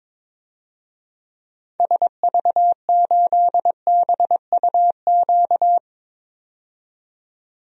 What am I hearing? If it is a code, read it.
Morse code, heard as a keyed tone, about 22 wpm, 700 Hz. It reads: SV8BUQ
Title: DXCC-100 Call Signs - 6 Characters - 22wpm